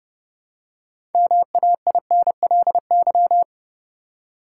Morse code, heard as a keyed tone, about 30 wpm, 700 Hz. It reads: MAINLY
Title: Six Letter Words - 30wpm